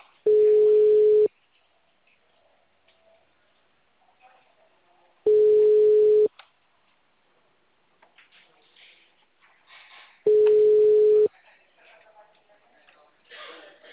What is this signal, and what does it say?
Ringback tone (Europe, 425 Hz), with speech.
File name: out- -2024- - - .188